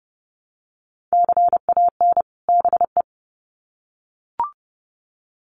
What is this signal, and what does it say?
Morse code, heard as a keyed tone, about 30 wpm, 700 Hz. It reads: CAN BE
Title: Sets of 2 Words 30wpm